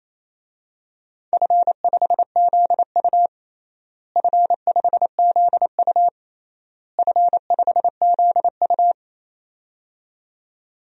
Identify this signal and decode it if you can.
Morse code, heard as a keyed tone, about 28 wpm, 700 Hz. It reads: F5ZU F5ZU F5ZU
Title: DXCC-100 Call Signs - 4 Characters - Repeated 3x - 28wpm